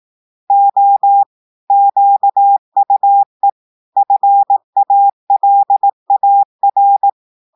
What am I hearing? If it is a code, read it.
Morse code, heard as a keyed tone, about 18 wpm, 800 Hz. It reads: O QUE FALAR
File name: audios / portugais / portugais texte en clair 2 vitesse 18 mots minutes